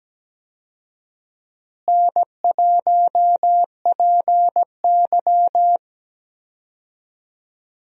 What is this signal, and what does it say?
Morse code, heard as a keyed tone, about 17 wpm, 700 Hz. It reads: N1PY
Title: USA Call Signs - 4 Characters - Repeated 2x - 17wpm